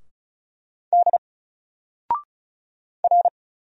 Morse code, heard as a keyed tone, about 36 wpm, 700 Hz. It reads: D R
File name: Single-Letter-Number-35wpm